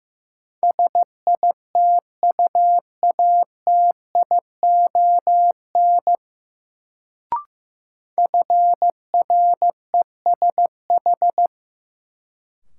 Morse code, heard as a keyed tone, about 15 wpm, 700 Hz. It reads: SITUATION FRESH
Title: Sentences from Top 1200 Words - New Word Drill - 15wpm